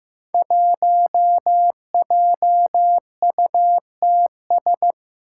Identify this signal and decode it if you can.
Morse code, heard as a keyed tone, about 15 wpm, 700 Hz. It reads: 1JUTS
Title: Five Challenging Letter-Number Combinations 15wpm